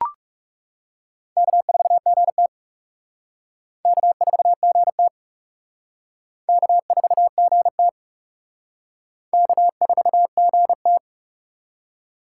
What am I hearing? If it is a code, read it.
Morse code, heard as a keyed tone, about 38 wpm, 700 Hz. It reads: K4GT K4GT K4GT K4GT